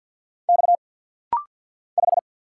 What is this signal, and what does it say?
Morse code, heard as a keyed tone, about 50 wpm, 700 Hz. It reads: X 5
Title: Single Letter-Number Alt - Rapid-Fire 50wpm